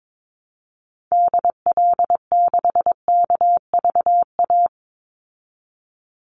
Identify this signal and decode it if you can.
Morse code, heard as a keyed tone, about 22 wpm, 700 Hz. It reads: DL6KVA
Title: Call Signs Repeated 2x - 22wpm